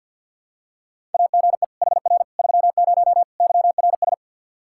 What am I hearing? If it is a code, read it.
Morse code, heard as a keyed tone, about 49 wpm, 700 Hz. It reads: AGE HR 30 YRS
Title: Practice QSO - Exchange 3 - More Info - 50wpm